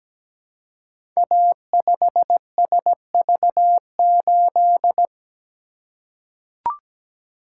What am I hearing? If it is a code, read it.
Morse code, heard as a keyed tone, about 17 wpm, 700 Hz. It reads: A5SV8
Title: Code Groups - 5 Letter:Numbers - 17wpm